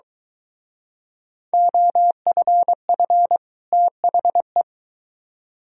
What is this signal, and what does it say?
Morse code, heard as a keyed tone, about 23 wpm, 700 Hz. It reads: OFF THE